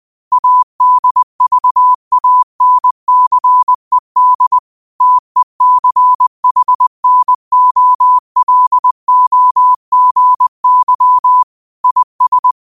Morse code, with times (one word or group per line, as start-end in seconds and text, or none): 0.3-4.6 ADVANCED
5.0-11.4 TECHNOLOGY
11.8-12.5 IS